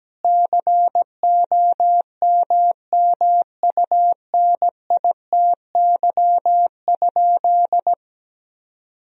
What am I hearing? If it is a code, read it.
Morse code, heard as a keyed tone, about 17 wpm, 700 Hz. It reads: COMMUNITY?
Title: Sentences from Top 1000 Words - Review Part 4 - 17wpm